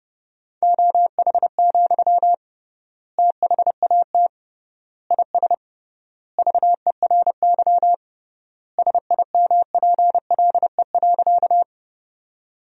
Morse code, heard as a keyed tone, about 32 wpm, 700 Hz. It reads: OH, THAT IS VERY SIMPLE.
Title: 3x Word-Spacing - Sentences from Top 500 Words - 30wpm